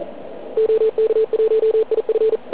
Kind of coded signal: Morse code